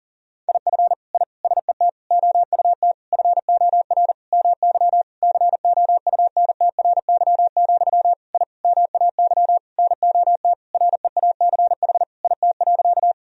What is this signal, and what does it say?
Morse code, heard as a keyed tone, about 40 wpm, 700 Hz. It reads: IF I SET OUT FOR MY COUNTRY, I MAY NOT REACH IT.
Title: Sentences from Top 1000 Words - Review Part 2 - 40wpm